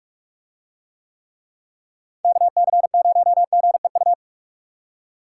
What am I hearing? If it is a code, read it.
Morse code, heard as a keyed tone, about 44 wpm, 700 Hz. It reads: KC0GEU